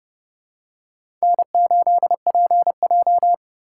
Morse code, heard as a keyed tone, about 30 wpm, 700 Hz. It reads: N8PJ
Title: USA Call Signs - 4 Characters - 2x Character Spacing - 30wpm